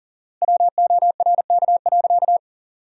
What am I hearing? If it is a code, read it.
Morse code, heard as a keyed tone, about 40 wpm, 700 Hz. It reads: WORK.
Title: 4x Word-Spacing - Sentences from Top 300 Words - 40wpm